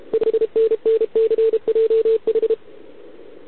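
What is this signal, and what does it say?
Morse code, heard as a keyed tone, about 32 wpm, 440 Hz. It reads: 5NNCJH
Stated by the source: Contest CW from OK1RI on 80 m during the OKOM DX contest, recorded by a QRP station using a KX3 and a wire antenna.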